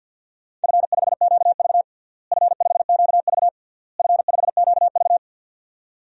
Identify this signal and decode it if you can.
Morse code, heard as a keyed tone, about 50 wpm, 700 Hz. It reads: F5QV F5QV F5QV
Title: DXCC-100 Call Signs - 4 Characters - Repeated 3x - 50wpm